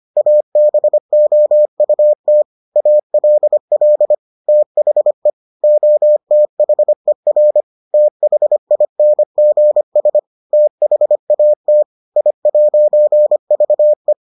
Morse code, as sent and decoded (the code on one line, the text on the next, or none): .- -... --- ..- - / .- .-.. .-.. / - .... / --- - .... .-. / - .... .. -. --. ... / - .... .- - / .. .----. ...- .
ABOUT ALL THE OTHER THINGS THAT I'VE